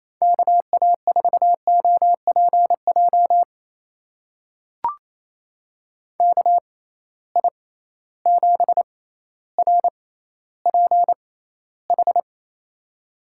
Morse code, heard as a keyed tone, about 28 wpm, spaced as for 14 wpm, 700 Hz. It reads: KA4OPJ KI7RPH
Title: USA Call Signs - 6 Characters - 6x Character Spacing - 28wpm